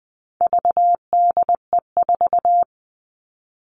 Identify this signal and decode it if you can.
Morse code, heard as a keyed tone, about 20 wpm, 700 Hz. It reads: VDE4
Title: All Cut Numbers - 4-Digit 20wpm